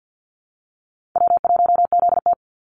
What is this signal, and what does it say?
Morse code, heard as a keyed tone, about 50 wpm, 700 Hz. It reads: W1ZT
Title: Call Signs Repeated 2x - 50wpm